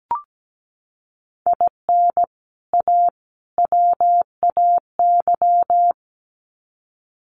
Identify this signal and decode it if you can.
Morse code, heard as a keyed tone, about 17 wpm, 700 Hz. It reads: IN A WAY
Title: Sets of 3 Words 17wpm